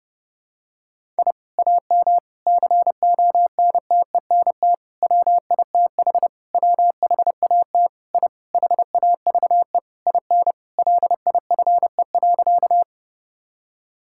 Morse code, heard as a keyed tone, about 30 wpm, 700 Hz. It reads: I AM CONTENT WITH WHAT I HAVE IN LIFE.
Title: Sentences from Top 1000 Words - Review Part 6 - 30wpm